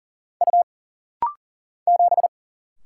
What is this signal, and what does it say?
Morse code, heard as a keyed tone, about 39 wpm, 700 Hz. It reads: U 7
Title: Single Letter-Number - Rapid-Fire 40wpm